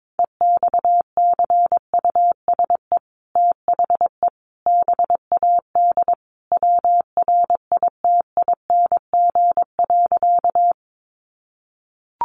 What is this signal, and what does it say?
Morse code, heard as a keyed tone, about 22 wpm, 700 Hz. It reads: EXCUSE THE BAD WRITING.
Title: Sentences from Top 800 Words 22wpm